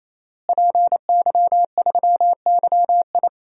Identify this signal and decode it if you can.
Morse code, heard as a keyed tone, about 28 wpm, 700 Hz. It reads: PY3YI
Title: DXCC-100 Call Signs - 5 Characters - Repeated 3x - 28wpm